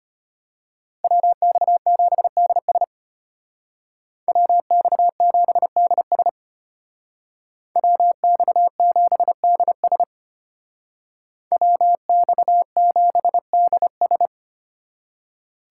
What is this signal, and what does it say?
Morse code, heard as a keyed tone, about 30 wpm, 700 Hz. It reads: WX7DS WX7DS WX7DS WX7DS
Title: Speed Racing - USA 2x2 Call Signs 25wpm